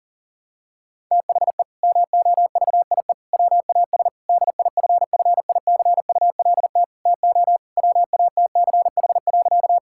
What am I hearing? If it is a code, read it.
Morse code, heard as a keyed tone, about 40 wpm, 700 Hz. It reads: THE MOVIE WAS DIFFICULT TO WATCH.
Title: Sentences from Top 1000 Words - Review Part 5 - 40wpm